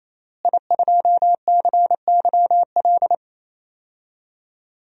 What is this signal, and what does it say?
Morse code, heard as a keyed tone, about 28 wpm, 700 Hz. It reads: I2CYL
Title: DXCC-100 Call Signs - 5 Characters - Repeated 3x - 28wpm